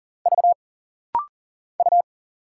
Morse code, heard as a keyed tone, about 40 wpm, 700 Hz. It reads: V U
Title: Single Letters - Rapid-Fire 40wpm